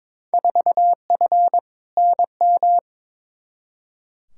Morse code, heard as a keyed tone, about 22 wpm, 700 Hz. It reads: 4F NM
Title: Field Day Exchanges Encore - 22wpm